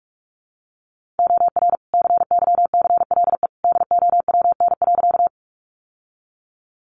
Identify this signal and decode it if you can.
Morse code, heard as a keyed tone, about 44 wpm, 700 Hz. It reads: OR CYCLE DOWN.
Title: Sentences from Top 900 Words 45wpm